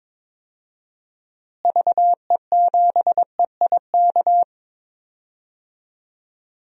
Morse code, heard as a keyed tone, about 22 wpm, 700 Hz. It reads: VE7EIK
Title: DXCC-100 Call Signs - 6 Characters - Repeated 3x - 22wpm